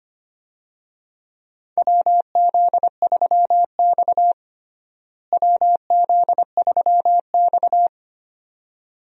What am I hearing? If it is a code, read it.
Morse code, heard as a keyed tone, about 25 wpm, 700 Hz. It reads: WZ3X WZ3X
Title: USA Call Signs - 4 Characters - Repeated 2x - 25wpm